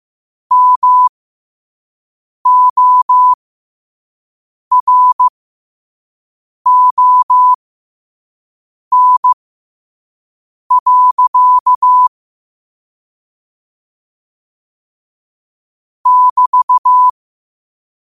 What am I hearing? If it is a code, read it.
Morse code, heard as a keyed tone, about 15 wpm, spaced as for 5 wpm, 1000 Hz. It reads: MORON. <BT>
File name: qotdfiles / 5wpm